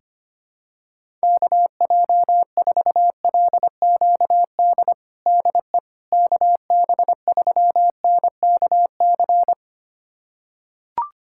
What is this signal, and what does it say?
Morse code, heard as a keyed tone, about 25 wpm, 700 Hz. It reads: KJ4LQD DE KB3NKC
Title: USA Callsign DE Callsign - 6 Characters - 25wpm